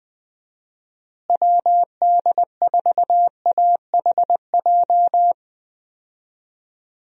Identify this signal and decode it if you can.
Morse code, heard as a keyed tone, about 20 wpm, 700 Hz. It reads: WD4AHJ
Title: USA Call Signs - 6 Characters - Repeated 2x - 20wpm